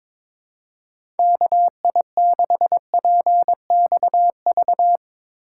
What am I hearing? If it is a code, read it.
Morse code, heard as a keyed tone, about 22 wpm, 700 Hz. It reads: KI6PXV
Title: USA Call Signs - 6 Characters - 2x Character Spacing - 22wpm